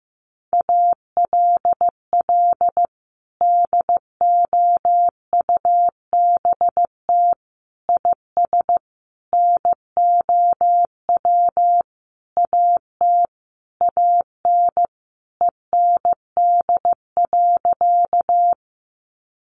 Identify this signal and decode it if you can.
Morse code, heard as a keyed tone, about 15 wpm, 700 Hz. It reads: ALL DOUBT IS NOW AT AN END.
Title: Sentences from Top 1500 Words 15wpm